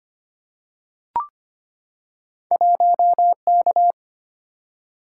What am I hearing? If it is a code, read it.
Morse code, heard as a keyed tone, about 25 wpm, 700 Hz. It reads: 1K
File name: Two Letter-Number Combinations - 25wpm